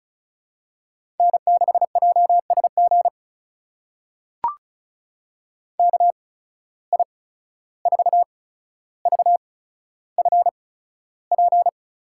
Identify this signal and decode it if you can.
Morse code, heard as a keyed tone, about 34 wpm, spaced as for 14 wpm, 700 Hz. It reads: N6JSG KI4VFP
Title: USA Call Signs - 8x Character Spacing - 35wpm